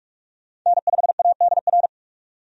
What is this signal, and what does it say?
Morse code, heard as a keyed tone, about 46 wpm, 700 Hz. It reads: N5ADR
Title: USA Call Signs - 5 Characters - 45wpm